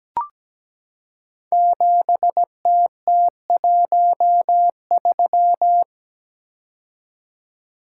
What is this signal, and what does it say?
Morse code, heard as a keyed tone, about 17 wpm, 700 Hz. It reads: 7TT13
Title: Code Groups - 5 Letter:Numbers - 17wpm